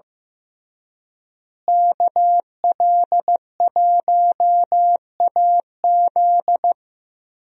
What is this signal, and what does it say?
Morse code, heard as a keyed tone, about 15 wpm, 700 Hz. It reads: KL1AZ
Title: USA Call Signs - 5 Characters - Repeated 3x - 15wpm